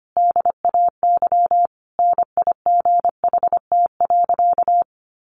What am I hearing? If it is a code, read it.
Morse code, heard as a keyed tone, about 25 wpm, 700 Hz. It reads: DAY NIGHT.
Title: A Sound Of Thunder 25wpm